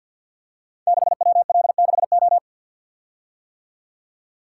Morse code, heard as a keyed tone, about 51 wpm, 690 Hz. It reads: 6WL6O
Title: Code Groups - 5 Letter:Numbers - 50wpm